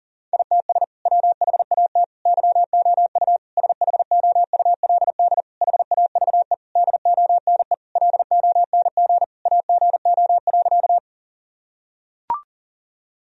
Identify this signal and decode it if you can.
Morse code, heard as a keyed tone, about 40 wpm, 700 Hz. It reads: ITS WHAT YOU SHOULD HAVE DONE LONG AGO.
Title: Sentences from Top 1000 Words - Review Part 3 - 40wpm